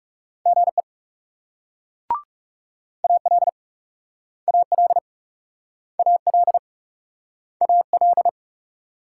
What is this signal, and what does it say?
Morse code, heard as a keyed tone, about 41 wpm, 700 Hz. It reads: GE AL AL AL AL